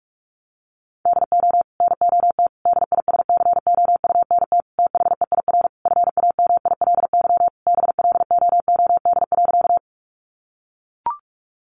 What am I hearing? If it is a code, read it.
Morse code, heard as a keyed tone, about 44 wpm, 700 Hz. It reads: DO NOT DISCOUNT THEIR FAMILY BLOOD.
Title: Sentences from Top 1200 Words 45wpm